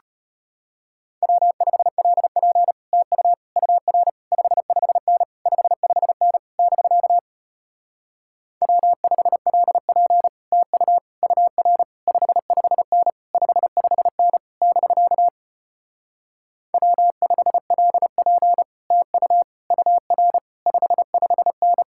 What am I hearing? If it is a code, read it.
Morse code, heard as a keyed tone, about 33 wpm, 700 Hz. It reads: W5LP TU UR 55N 55N <BK> W5LP TU UR 55N 55N <BK> W5LP TU UR 55N